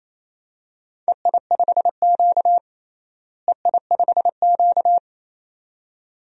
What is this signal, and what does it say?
Morse code, heard as a keyed tone, about 28 wpm, 700 Hz. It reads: EI5Q EI5Q
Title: DXCC-100 Call Signs - 4 Characters - Repeated 2x - 28wpm